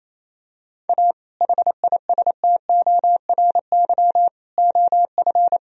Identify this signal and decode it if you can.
Morse code, heard as a keyed tone, about 28 wpm, 700 Hz. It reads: A HISTORY OF
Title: Sentences from Top 1000 Words - Review Part 3 - 28wpm